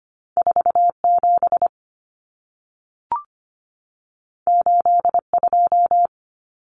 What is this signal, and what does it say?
Morse code, heard as a keyed tone, about 25 wpm, 700 Hz. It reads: 47 82
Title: Common Cut Numbers - 2-Digit 25wpm